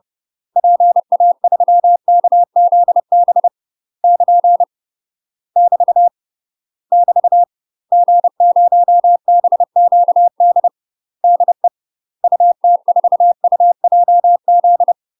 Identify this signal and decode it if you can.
Morse code, heard as a keyed tone, about 30 wpm, 700 Hz. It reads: PA3KZB <KN> <BT> <BT> G0BQD DE UT4UJZ
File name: UK MASTER QSOs TRAINING / 30WPM / UK MASTER QSOs 0026